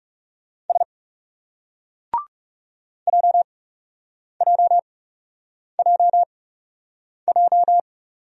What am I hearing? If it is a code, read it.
Morse code, heard as a keyed tone, about 38 wpm, spaced as for 8 wpm, 700 Hz. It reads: S JJJJ